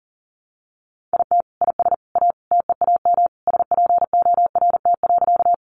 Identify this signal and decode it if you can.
Morse code, heard as a keyed tone, about 40 wpm, 700 Hz. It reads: IT IS A TEAM SPORT.
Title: Sentences from Top 1500 Words 40wpm